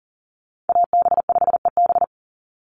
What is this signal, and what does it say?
Morse code, heard as a keyed tone, about 40 wpm, 700 Hz. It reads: AB5EB
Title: Call Signs - 40wpm